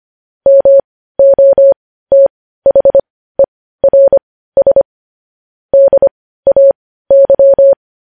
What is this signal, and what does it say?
Morse code, heard as a keyed tone, about 25 wpm, spaced as for 15 wpm, 550 Hz. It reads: MOTHERS DAY
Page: news-headlines-in-morse-code-at-15-wpm